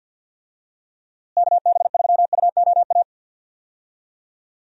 Morse code, heard as a keyed tone, about 51 wpm, 700 Hz. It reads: KB3UOA
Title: USA Call Signs - 6 Characters - Repeated 2x - 50wpm